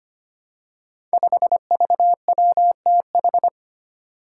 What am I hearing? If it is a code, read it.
Morse code, heard as a keyed tone, about 25 wpm, 700 Hz. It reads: 5VWTH
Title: Five Challenging Letter-Number Combinations 25wpm